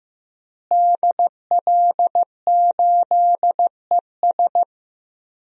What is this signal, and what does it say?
Morse code, heard as a keyed tone, about 15 wpm, 700 Hz. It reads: DL8ES
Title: DXCC-100 Call Signs - 5 Characters - Repeated 3x - 15wpm